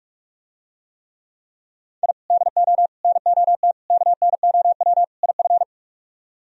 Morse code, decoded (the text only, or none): I DO NOT KNOW IF